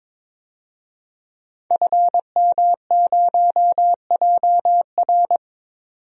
Morse code, heard as a keyed tone, about 22 wpm, 700 Hz. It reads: FM0JR
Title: DXCC-100 Call Signs - 5 Characters - 22wpm